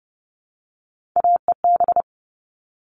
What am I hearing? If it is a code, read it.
Morse code, heard as a keyed tone, about 29 wpm, 700 Hz. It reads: AEB
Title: All Cut Numbers - 3-Digit 30wpm